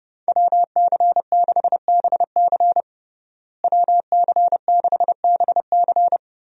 Morse code, heard as a keyed tone, about 30 wpm, 700 Hz. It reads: WC6BC WC6BC
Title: USA Call Signs - 5 Characters - Repeated 3x - 30wpm